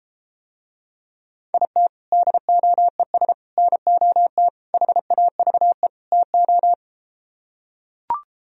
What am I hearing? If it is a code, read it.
Morse code, heard as a keyed tone, about 33 wpm, 700 Hz. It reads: IT DOES NOT HAVE TO